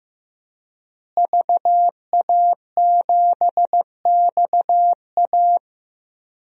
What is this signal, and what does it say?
Morse code, heard as a keyed tone, about 15 wpm, 700 Hz. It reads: VA7XA
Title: DXCC-100 Call Signs - 5 Characters - Repeated 3x - 15wpm